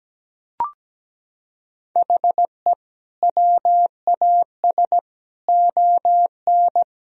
Morse code, heard as a keyed tone, about 17 wpm, 700 Hz. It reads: HE WAS ON
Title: Sentences from Top 1000 Words - Review Part 3 - 17wpm